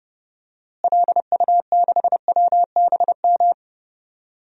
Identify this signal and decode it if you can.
Morse code, heard as a keyed tone, about 30 wpm, 700 Hz. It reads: LU6WBM
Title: DXCC-100 Call Signs - 6 Characters - Repeated 2x - 30wpm